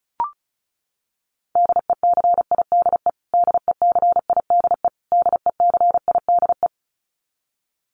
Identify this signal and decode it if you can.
Morse code, heard as a keyed tone, about 34 wpm, 700 Hz. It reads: DECIDE DECIDE DECIDE
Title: Top 500 Words Repeated 35wpm